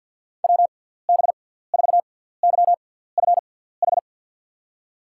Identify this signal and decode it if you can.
Morse code, heard as a keyed tone, about 48 wpm, spaced as for 17 wpm, 700 Hz. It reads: WB4YFH
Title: USA Call Signs - 6 Characters - 6x Character Spacing - 50wpm